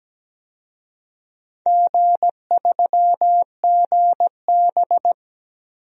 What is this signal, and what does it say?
Morse code, heard as a keyed tone, about 17 wpm, 700 Hz. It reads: G3GB
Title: DXCC-100 Call Signs - 4 Characters - 17wpm